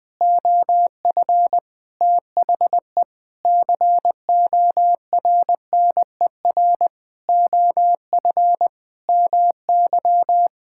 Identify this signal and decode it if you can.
Morse code, heard as a keyed tone, about 20 wpm, 700 Hz. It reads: OF THE CORNER OF MY
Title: Sets of 8 Words - 20wpm